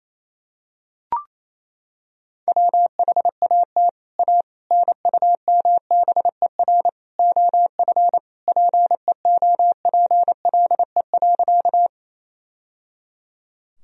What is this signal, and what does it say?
Morse code, heard as a keyed tone, about 28 wpm, 700 Hz. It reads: WHAT A NUMBER OF PEOPLE.
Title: Sentences from Top 1000 Words - Review Part 1 - 28wpm